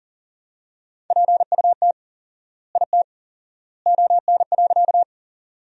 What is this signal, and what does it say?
Morse code, heard as a keyed tone, about 39 wpm, 700 Hz. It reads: PUT IT ON.
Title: 4x Word-Spacing - Sentences from Top 200 Words - 40wpm